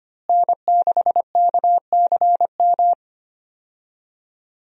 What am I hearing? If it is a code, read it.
Morse code, heard as a keyed tone, about 25 wpm, 700 Hz. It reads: N6KCM
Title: USA Call Signs - 5 Characters - Repeated 3x - 25wpm